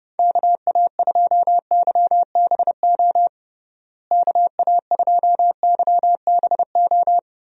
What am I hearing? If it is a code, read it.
Morse code, heard as a keyed tone, about 31 wpm, 700 Hz. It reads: KA2YBO KA2YBO
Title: USA Call Signs - 6 Characters - Repeated 2x - 30wpm